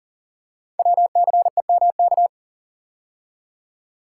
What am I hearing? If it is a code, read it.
Morse code, heard as a keyed tone, about 42 wpm, 690 Hz. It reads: WCEMK
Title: Code Groups - 5 Letters - 40wpm